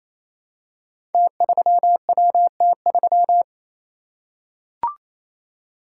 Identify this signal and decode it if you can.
Morse code, heard as a keyed tone, about 28 wpm, 690 Hz. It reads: T3WT3
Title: Code Groups - 5 Letter:Numbers - 28wpm